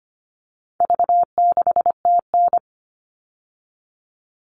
Morse code, heard as a keyed tone, about 25 wpm, 700 Hz. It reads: V6TN
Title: All Cut Numbers - 4-Digit 25wpm